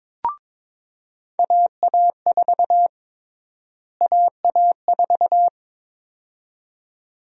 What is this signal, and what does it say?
Morse code, heard as a keyed tone, about 22 wpm, 700 Hz. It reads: AA4 AA4
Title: USA Call Sign Prefixes - Repeated 2x - 22wpm